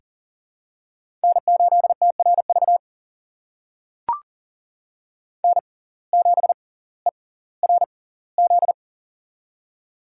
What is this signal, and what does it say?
Morse code, heard as a keyed tone, about 40 wpm, spaced as for 20 wpm, 700 Hz. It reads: N8TRV N7ERZ
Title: USA Call Signs - 5 Characters - 6x Character Spacing - 40wpm